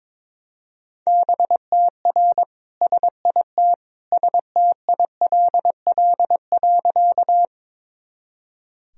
Morse code, heard as a keyed tone, about 22 wpm, 700 Hz. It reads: BTR SIT STILL.
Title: Sentences from Top 500 Words and Abbreviations 22wpm